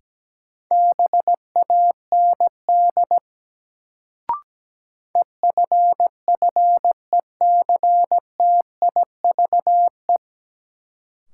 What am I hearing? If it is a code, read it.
Morse code, heard as a keyed tone, about 17 wpm, 700 Hz. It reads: BAND EFFECTIVE
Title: Sentences from Top 1600 Words - New Word Drill Alt - 17wpm